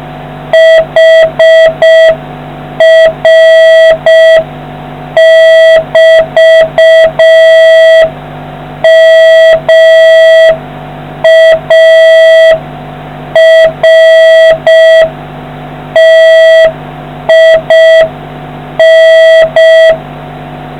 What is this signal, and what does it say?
Morse code, heard as a keyed tone, about 6 wpm, 650 Hz, with noise. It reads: HR<BT>MARTIN